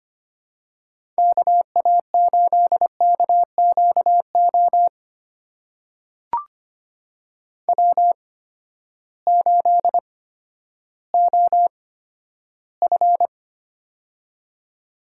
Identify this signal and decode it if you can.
Morse code, heard as a keyed tone, about 25 wpm, spaced as for 12 wpm, 700 Hz. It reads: KA8KQO W8OF
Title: USA Call Signs - 8x Character Spacing - 25wpm